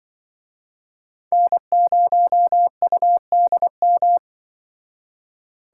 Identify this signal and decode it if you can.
Morse code, heard as a keyed tone, about 25 wpm, 700 Hz. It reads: N0UDM
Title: Speed Racing - USA 1x3 Call Signs 20wpm